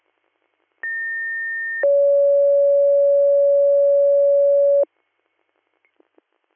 Two-tone page, heard as a one-second tone, then a three-second tone, about 1800 then 560 Hz. It is heard from 0.8 to 4.8 s.